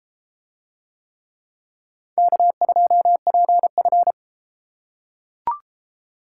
Morse code, heard as a keyed tone, about 31 wpm, 700 Hz. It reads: K2PF